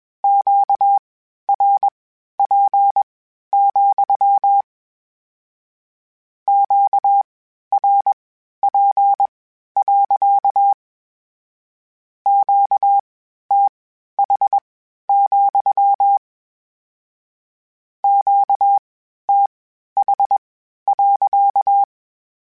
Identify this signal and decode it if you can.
Morse code, heard as a keyed tone, about 21 wpm, spaced as for 12 wpm, 800 Hz. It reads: QRP, QRP. QTH, QTH.
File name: Q Calls / Q Calls-mp3s / Q Calls sppc @20